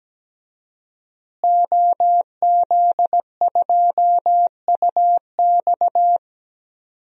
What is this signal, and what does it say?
Morse code, heard as a keyed tone, about 17 wpm, 700 Hz. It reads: OZ2UX
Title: DXCC-100 Call Signs - 5 Characters - Repeated 3x - 17wpm